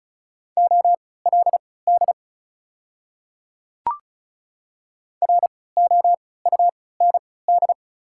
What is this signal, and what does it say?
Morse code, heard as a keyed tone, about 36 wpm, spaced as for 20 wpm, 700 Hz. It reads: OLD ROUND
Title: 3x Character Spacing - Top 200 Words - 35wpm